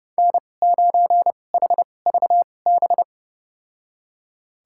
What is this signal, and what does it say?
Morse code, heard as a keyed tone, about 30 wpm, spaced as for 22 wpm, 700 Hz. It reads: N9HVB
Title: USA Call Signs - 5 Characters - 2x Character Spacing - 30wpm